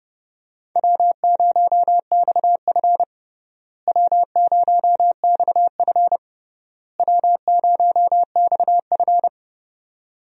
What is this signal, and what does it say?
Morse code, heard as a keyed tone, about 30 wpm, 700 Hz. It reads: W0XF W0XF W0XF
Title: USA Call Signs - 4 Characters - Repeated 3x - 30wpm